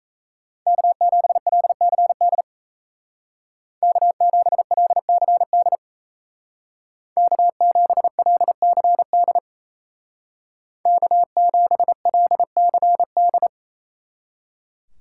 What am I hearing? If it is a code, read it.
Morse code, heard as a keyed tone, about 34 wpm, 700 Hz. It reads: K7LCD K7LCD K7LCD K7LCD
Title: Speed Racing - USA 1x3 Call Signs 28wpm